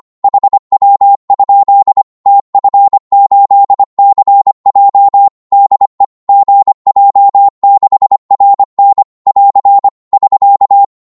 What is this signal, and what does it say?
Morse code, heard as a keyed tone, about 25 wpm, 800 Hz. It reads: HW? TF8CJ DE GJ6RN <AR> <SK>
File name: Morse / qso-25wpm